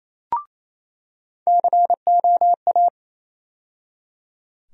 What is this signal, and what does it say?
Morse code, heard as a keyed tone, about 28 wpm, 700 Hz. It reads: COA
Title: USA Call Sign Postfixes - 3 Letters 28wpm